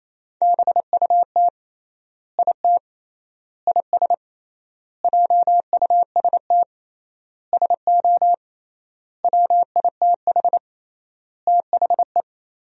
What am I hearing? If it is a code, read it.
Morse code, heard as a keyed tone, about 28 wpm, 700 Hz. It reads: BUT IT IS JUST SO WITH THE
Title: 3x Word-Spacing - Sentences from Top 200 Words - 28wpm